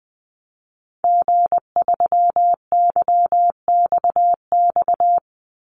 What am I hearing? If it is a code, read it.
Morse code, heard as a keyed tone, about 19 wpm, 700 Hz. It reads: G3YXX
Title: Call Signs - 20wpm